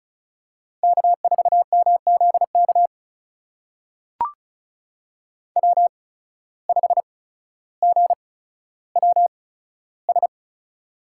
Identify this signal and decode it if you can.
Morse code, heard as a keyed tone, about 36 wpm, spaced as for 15 wpm, 700 Hz. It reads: K4MZK W5GWS
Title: USA Call Signs - 5 Characters - 8x Character Spacing - 35wpm